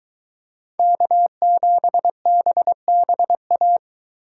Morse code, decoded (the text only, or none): K7BBA